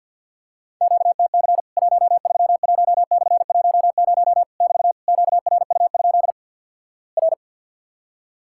Morse code, heard as a keyed tone, about 50 wpm, 700 Hz. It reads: QTC 131/10 <BT> QRU? R
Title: Worked All Europe DX Contest - QTC Exchange with Repeats - 50wpm